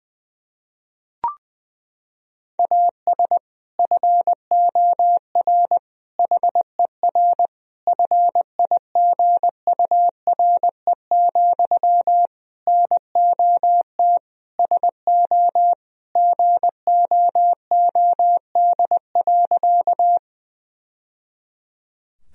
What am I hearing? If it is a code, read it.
Morse code, heard as a keyed tone, about 20 wpm, 700 Hz. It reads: AS FOR HER FIGURE, NOT SO GOOD.
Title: Sentences from Top 1000 Words - Review Part 2 - 20wpm